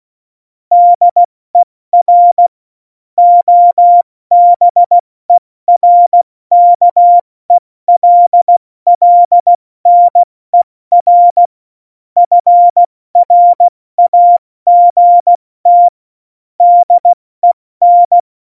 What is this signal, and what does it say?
Morse code, heard as a keyed tone, about 16 wpm, 700 Hz. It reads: DER OBERKELLNER FRAGT DEN
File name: Morsekurs / Witze / Witz 10. September 2019, 70 F80